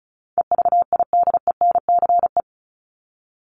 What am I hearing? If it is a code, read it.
Morse code, heard as a keyed tone, about 34 wpm, 700 Hz. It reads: EVIDENCE
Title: Top 1001-1250 Words 35wpm